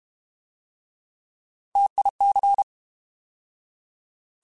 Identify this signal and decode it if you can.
Morse code, heard as a keyed tone, about 32 wpm, 800 Hz. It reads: TIC